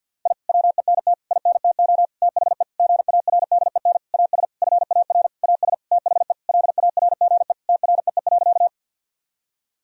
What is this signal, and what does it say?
Morse code, heard as a keyed tone, about 52 wpm, 700 Hz. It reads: I WENT INTO THE GARDEN AS FAR AS THE LARGE TREE.
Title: Sentences from Top 1000 Words - Review Part 4 - 50wpm